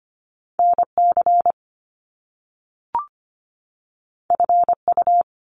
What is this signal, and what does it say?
Morse code, heard as a keyed tone, about 25 wpm, 700 Hz. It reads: NC FU
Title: Top Two Letter Combinations in Words 25wpm